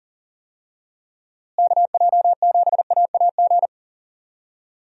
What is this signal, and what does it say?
Morse code, heard as a keyed tone, about 40 wpm, 700 Hz. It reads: KJ7AAG
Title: USA Call Signs - 6 Characters - Repeated 2x - 40wpm